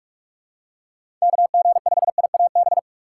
Morse code, heard as a keyed tone, about 43 wpm, 700 Hz. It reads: KG5IAB
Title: USA Call Signs - 4x Character Spacing - 45wpm